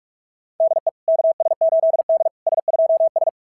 Morse code, heard as a keyed tone, about 44 wpm, 660 Hz. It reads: DE KS8D S2S
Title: SOTA QSO Practice 45wpm